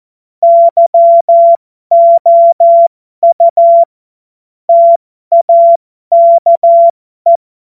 Morse code, heard as a keyed tone, about 14 wpm, spaced as for 12 wpm, 680 Hz. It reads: YOU TAKE
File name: Morsekurs / Witze